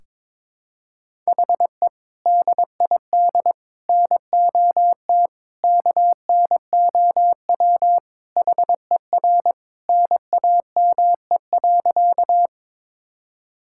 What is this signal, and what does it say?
Morse code, heard as a keyed tone, about 22 wpm, 700 Hz. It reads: HE DID NOT KNOW HER NAME.
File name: practice / 22wpm / Sentences with 4-Word Phrases - 22wpm